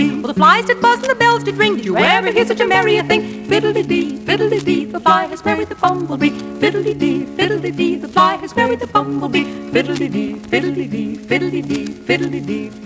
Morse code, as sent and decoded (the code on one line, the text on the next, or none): none